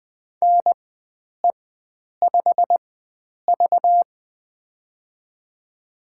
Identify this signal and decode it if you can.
Morse code, heard as a keyed tone, about 20 wpm, spaced as for 9 wpm, 700 Hz. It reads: NE5V
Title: USA Call Signs - 4 Characters - 4x Character Spacing - 20wpm